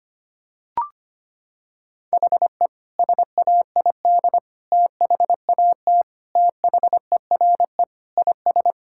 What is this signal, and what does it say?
Morse code, heard as a keyed tone, about 25 wpm, 700 Hz. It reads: HE SAID THAT THERE IS
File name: practice / 25wpm / Sentences with 4-Word Phrases - 25wpm